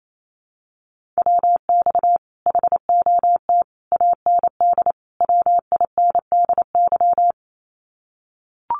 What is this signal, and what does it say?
Morse code, heard as a keyed tone, about 28 wpm, 700 Hz. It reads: WX HOT AND WINDY
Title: Common QSO Phrases Encore 28wpm